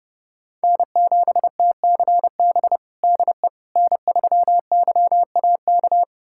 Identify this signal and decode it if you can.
Morse code, heard as a keyed tone, about 32 wpm, 700 Hz. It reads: N7TCB DE N3YAK
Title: USA Callsign DE Callsign - 5 Characters - 30wpm